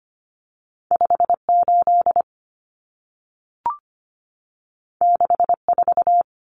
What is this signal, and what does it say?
Morse code, heard as a keyed tone, about 25 wpm, 700 Hz. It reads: 58 64
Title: Common Cut Numbers - 2-Digit 25wpm